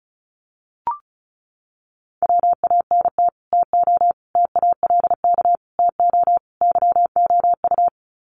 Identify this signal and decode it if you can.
Morse code, heard as a keyed tone, about 34 wpm, 700 Hz. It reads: WANT TO TALK TO YOU
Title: Sets of 5 Words Encore 35wpm